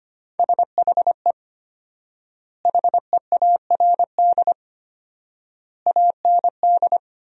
Morse code, heard as a keyed tone, about 25 wpm, 700 Hz. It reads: SHE HEARD AND
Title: 4x Word-Spacing - Sentences from Top 500 Words - 25wpm